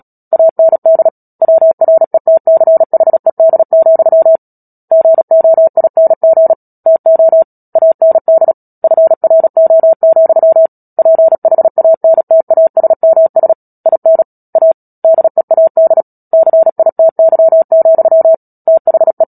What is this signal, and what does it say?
Morse code, heard as a keyed tone, about 37 wpm, 650 Hz. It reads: AND WRETCHED, GOING TO AND FRO, PHANTASMS IN A DEAD CITY, THE
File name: Book Two / 10-Chapter 10 - Epilogue - 36 WPM